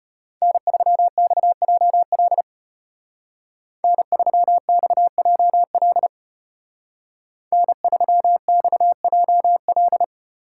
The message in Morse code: N3XJL N3XJL N3XJL

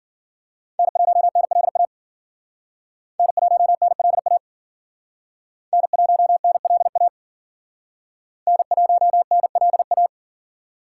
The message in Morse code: N1NLA N1NLA N1NLA N1NLA

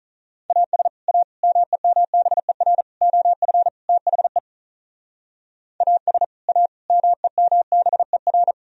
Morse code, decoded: AS A MEMBER OF THE AS A MEMBER

